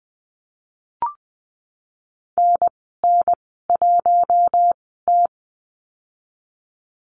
Morse code, heard as a keyed tone, about 20 wpm, spaced as for 15 wpm, 700 Hz. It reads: NN1T
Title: 048 Lesson 9g - Number 4 - Call Signs 20wpm